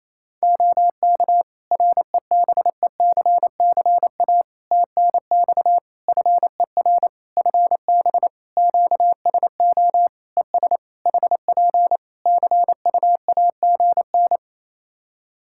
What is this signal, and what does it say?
Morse code, heard as a keyed tone, about 28 wpm, 700 Hz. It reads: OK REBECCA TNX FER FB QSO ES HP CUAGN
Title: Practice QSO - Exchange 4 - Ending - 28wpm